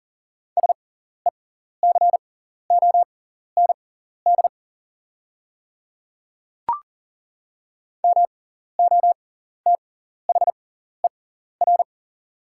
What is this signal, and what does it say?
Morse code, heard as a keyed tone, about 37 wpm, spaced as for 14 wpm, 700 Hz. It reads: SECOND MOTHER
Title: 6x Character Spacing - Top 300 Words - 40wpm